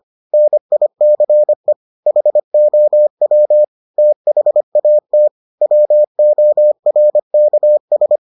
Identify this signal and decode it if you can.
Morse code, heard as a keyed tone, about 25 wpm, 600 Hz. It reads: NICE HOW THAT WORKS